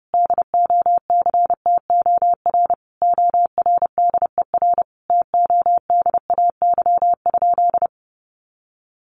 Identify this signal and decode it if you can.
Morse code, heard as a keyed tone, about 30 wpm, 700 Hz. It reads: DOCTOR ORDER TODAY?
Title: Sentences from Top 1500 Words 30wpm